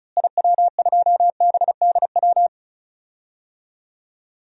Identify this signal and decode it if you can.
Morse code, heard as a keyed tone, about 36 wpm, 700 Hz. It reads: IW2BDW
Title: DXCC-100 Call Signs - 6 Characters - Repeated 3x - 35wpm